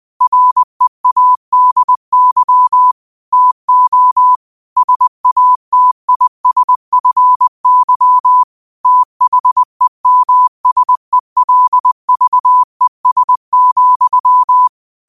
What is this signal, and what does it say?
Morse code, heard as a keyed tone, about 20 wpm, 1000 Hz. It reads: READY TO SATISFY THEMSELVES,
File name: qotdfiles / 20wpm